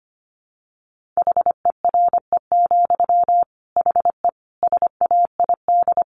Morse code, heard as a keyed tone, about 25 wpm, 700 Hz. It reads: HERE, HE SAID
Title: Sentences from Top 1500 Words 25wpm